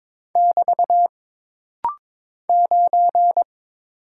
Morse code, heard as a keyed tone, about 22 wpm, 700 Hz. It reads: <BT> 9
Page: This rapid-fire timing is geared to push your ICR (Instant Character Recognition) skills.